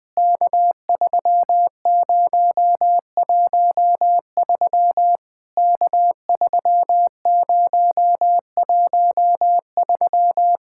Morse code, heard as a keyed tone, about 20 wpm, 700 Hz. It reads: K3013 K3013
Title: POTA QSO Practice - 20wpm